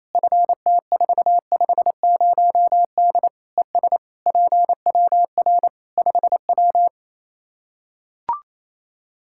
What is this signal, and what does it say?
Morse code, heard as a keyed tone, about 28 wpm, 700 Hz. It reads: FT450D ES PWR 5W
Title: Practice QSO - Exchange 2 - Info - 28wpm